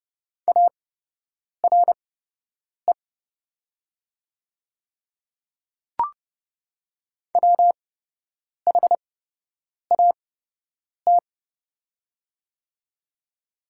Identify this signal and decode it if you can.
Morse code, heard as a keyed tone, about 30 wpm, spaced as for 8 wpm, 700 Hz. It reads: ARE WHAT
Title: 8x Character Spacing - Top 200 Words - 30wpm